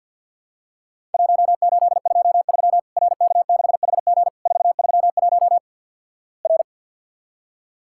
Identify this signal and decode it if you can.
Morse code, heard as a keyed tone, about 50 wpm, 700 Hz. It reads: 1823 RK6HG 431 R
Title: Worked All Europe DX Contest - QTC Exchange with Repeats - 50wpm